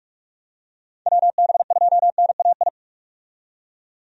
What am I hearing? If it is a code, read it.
Morse code, heard as a keyed tone, about 46 wpm, 700 Hz. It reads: WB2NAI
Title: USA Call Signs - 6 Characters - 45wpm